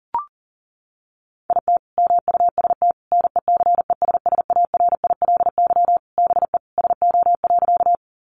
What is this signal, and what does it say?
Morse code, heard as a keyed tone, about 40 wpm, 700 Hz. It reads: IT MUST NECESSARILY BE SO.
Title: Sentences from Top 1400 Words 40wpm